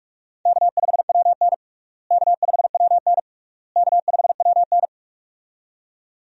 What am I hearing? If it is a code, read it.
Morse code, heard as a keyed tone, about 45 wpm, 700 Hz. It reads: K5WN K5WN K5WN
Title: USA Call Signs - 4 Characters - Repeated 3x - 45wpm